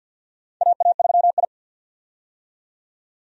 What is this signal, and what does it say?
Morse code, heard as a keyed tone, about 51 wpm, 700 Hz. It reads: AA3I
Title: USA Call Signs - 4 Characters - 50wpm